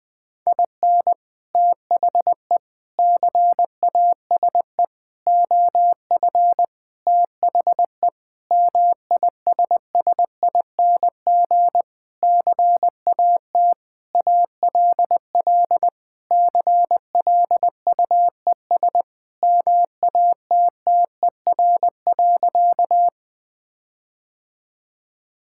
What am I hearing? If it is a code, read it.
Morse code, heard as a keyed tone, about 20 wpm, 700 Hz. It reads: IN THE CASE OF THE MISSING CAT ALL CLUES MATTER.
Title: Sentences with 4-Word Phrases - 20wpm